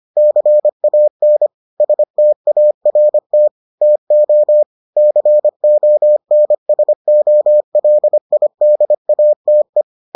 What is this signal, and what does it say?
Morse code, heard as a keyed tone, about 25 wpm, 600 Hz. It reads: CAN START TO CONSOLIDATE